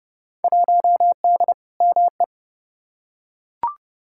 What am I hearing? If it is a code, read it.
Morse code, heard as a keyed tone, about 30 wpm, 700 Hz. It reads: 1D ME
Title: Field Day Exchanges - 30wpm